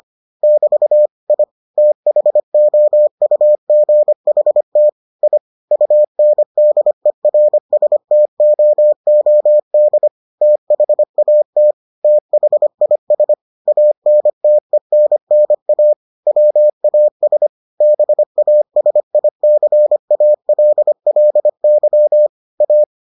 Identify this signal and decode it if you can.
Morse code, heard as a keyed tone, about 25 wpm, 600 Hz. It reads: <BT> I THOUGHT I UNDERSTOOD THAT THIS ANTENNA WAS BASICALLY A